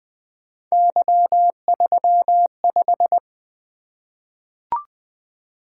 Morse code, heard as a keyed tone, about 20 wpm, 700 Hz. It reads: Y35
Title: Three Letter-Number Combinations - 20wpm